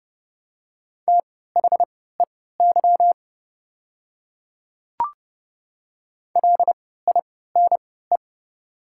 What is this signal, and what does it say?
Morse code, heard as a keyed tone, about 29 wpm, spaced as for 17 wpm, 700 Hz. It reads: THEY LINE
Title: 3x Character Spacing - Top 200 Words - 30wpm